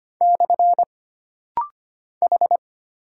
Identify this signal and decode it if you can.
Morse code, heard as a keyed tone, about 25 wpm, 700 Hz. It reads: / H
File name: Single Character (Letter:Number:Punctuation:Prosign) - Rapid-Fire - 25wpm